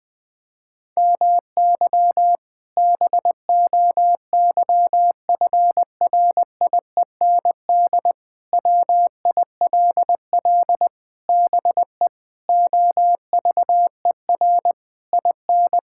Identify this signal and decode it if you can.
Morse code, heard as a keyed tone, about 21 wpm, 700 Hz. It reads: MY BOYFRIEND WILL BE OVER IN
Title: Sentences from Top 1000 Words - Review Part 5 - 20wpm